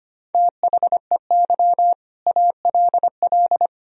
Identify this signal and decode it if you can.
Morse code, heard as a keyed tone, about 25 wpm, 700 Hz. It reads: THEY ALL
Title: Sentences from Top 500 Words and Abbreviations 25wpm